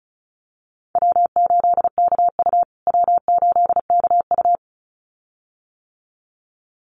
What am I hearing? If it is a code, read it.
Morse code, heard as a keyed tone, about 34 wpm, 700 Hz. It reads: W8KU W8KU
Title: USA Call Signs Repeated 2x 35wpm